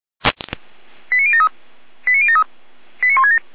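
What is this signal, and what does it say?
Five-tone selective call (ZVEI), heard as quick runs of five tones, tones of about 70 ms. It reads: 89073 89073 89167
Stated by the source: Five-tone ZVEI selective call received on an SDR.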